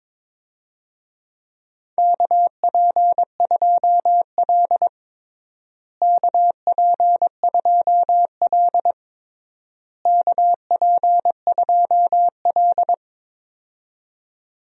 Morse code, heard as a keyed tone, about 22 wpm, 700 Hz. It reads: KP2L KP2L KP2L
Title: DXCC-100 Call Signs - 4 Characters - Repeated 3x - 22wpm